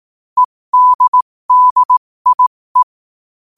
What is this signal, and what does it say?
Morse code, heard as a keyed tone, about 18 wpm, spaced as for 15 wpm, 1000 Hz. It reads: EDDIE
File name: qotdfiles / 15wpm